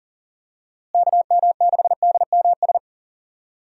Morse code, heard as a keyed tone, about 40 wpm, 700 Hz. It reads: KM6DMS
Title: USA Call Signs - 6 Characters - Repeated 3x - 40wpm